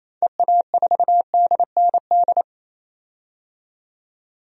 Morse code, heard as a keyed tone, about 28 wpm, 700 Hz. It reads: EA4DND
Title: DXCC-100 Call Signs - 6 Characters - 28wpm